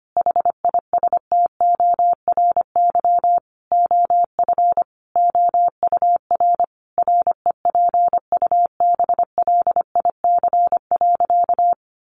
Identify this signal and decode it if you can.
Morse code, heard as a keyed tone, about 25 wpm, 700 Hz. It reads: HISTORY OF OUR REPUBLIC.
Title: Sentences from Top 1500 Words 25wpm